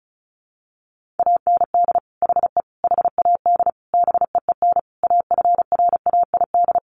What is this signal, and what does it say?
Morse code, heard as a keyed tone, about 34 wpm, 700 Hz. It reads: AND HE HAD BEEN AFRAID
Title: Sentences from Top 1500 Words 35wpm